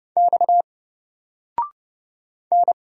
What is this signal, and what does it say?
Morse code, heard as a keyed tone, about 30 wpm, 700 Hz. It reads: X N
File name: Single-Letter-Number-30wpm